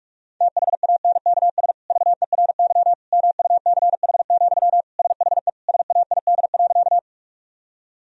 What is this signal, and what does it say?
Morse code, heard as a keyed tone, about 46 wpm, 700 Hz. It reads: THANKS VERY MUCH, SHE SAID.
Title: Sentences from Top 1000 Words - Review Part 3 - 45wpm